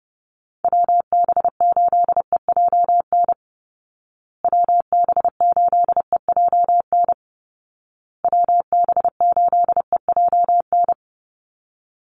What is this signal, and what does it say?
Morse code, heard as a keyed tone, about 30 wpm, 700 Hz. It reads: WB8EJN WB8EJN WB8EJN
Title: Call Signs Repeated 2x - 30wpm